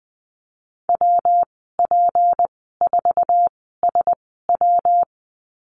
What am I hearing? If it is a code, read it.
Morse code, heard as a keyed tone, about 19 wpm, spaced as for 14 wpm, 700 Hz. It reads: WP4SW
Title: 167 Lesson 27g - Number 7 - Call Signs 20wpm